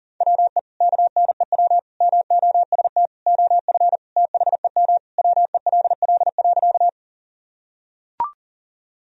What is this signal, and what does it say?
Morse code, heard as a keyed tone, about 40 wpm, 700 Hz. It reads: WE KNEW MOST OF THEM WELL.
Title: Sentences from Top 1000 Words - Review Part 1 - 40wpm